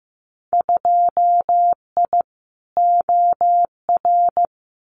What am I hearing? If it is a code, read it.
Morse code, heard as a keyed tone, about 15 wpm, 700 Hz. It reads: 2I OR
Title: Winter Field Day Exchanges Encore 15wpm